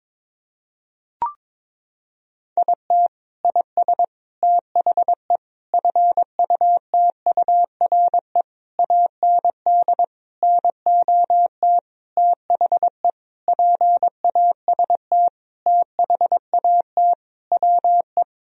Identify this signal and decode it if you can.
Morse code, heard as a keyed tone, about 21 wpm, 700 Hz. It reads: IT IS THE FUTURE AND NOT THE PAST THAT WE
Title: Sentences from Top 1000 Words - Review Part 4 - 22wpm